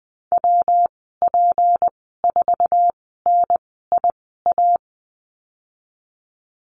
Morse code, heard as a keyed tone, about 20 wpm, spaced as for 15 wpm, 700 Hz. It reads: WP4NIA
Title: 186 Lesson 30g - Letter b - Call Signs 20wpm